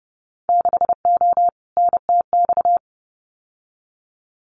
Morse code, heard as a keyed tone, about 30 wpm, 700 Hz. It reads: 6O NTX
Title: Winter Field Day Exchanges Encore 30wpm